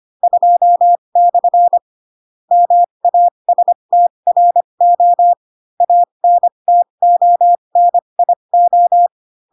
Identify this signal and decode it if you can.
Morse code, heard as a keyed tone, about 25 wpm, spaced as for 22 wpm, 700 Hz. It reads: 2/ MASTRO ANTONIO
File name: Pinocchio / Pinocchio ch2 18 25 ik1hge